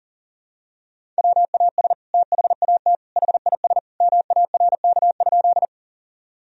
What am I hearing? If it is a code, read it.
Morse code, heard as a keyed tone, about 40 wpm, 700 Hz. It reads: WAS THAT HIS MARK?
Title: Sentences from Top 1000 Words - Review Part 1 - 40wpm